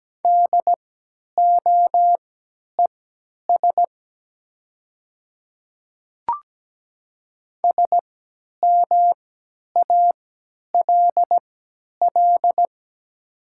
Morse code, heard as a keyed tone, about 17 wpm, spaced as for 10 wpm, 700 Hz. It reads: DOES SMALL
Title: 3x Character Spacing - Top 300 Words - 17wpm